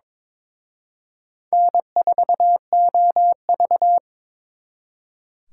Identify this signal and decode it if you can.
Morse code, heard as a keyed tone, about 26 wpm, 700 Hz. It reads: N4OV
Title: Speed Racing - USA Call Signs 22wpm